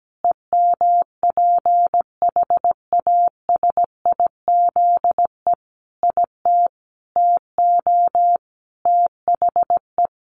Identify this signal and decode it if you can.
Morse code, heard as a keyed tone, about 17 wpm, 700 Hz. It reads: EMPHASIZE IT TO THE
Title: Sentences from Top 1500 Words 17wpm